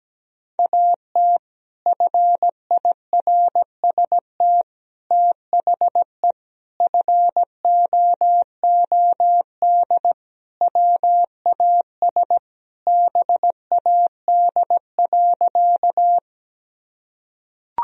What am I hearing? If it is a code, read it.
Morse code, heard as a keyed tone, about 17 wpm, 700 Hz. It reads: AT FIRST THE FOOD WAS BAD.
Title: Sentences from Top 1000 Words - Review Part 1 - 17wpm